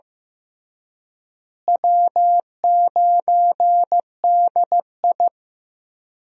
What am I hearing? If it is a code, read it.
Morse code, heard as a keyed tone, about 15 wpm, 700 Hz. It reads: W9DI
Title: USA Call Signs - 4 Characters - Repeated 3x - 15wpm